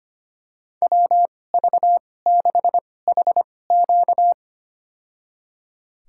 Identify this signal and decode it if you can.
Morse code, heard as a keyed tone, about 25 wpm, spaced as for 18 wpm, 700 Hz. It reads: WV6HQ
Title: USA Call Signs - 5 Characters - 2x Character Spacing - 25wpm